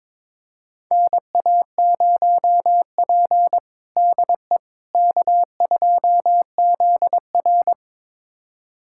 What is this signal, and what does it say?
Morse code, heard as a keyed tone, about 22 wpm, 700 Hz. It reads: NA0P DE K2ZR
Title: USA Callsign DE Callsign - 4 Characters - 22wpm